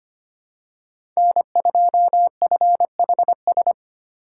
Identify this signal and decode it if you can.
Morse code, heard as a keyed tone, about 25 wpm, 700 Hz. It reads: N2FHS
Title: USA Call Signs - 3x Character Spacing - 25wpm